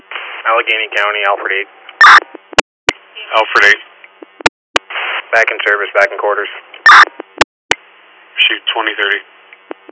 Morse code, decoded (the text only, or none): none